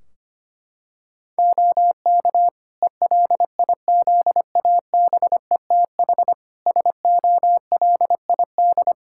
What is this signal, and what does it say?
Morse code, heard as a keyed tone, about 25 wpm, 700 Hz. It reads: OK ELIZABETH SOLID
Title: Practice QSO - Exchange 3 - More Info - 25wpm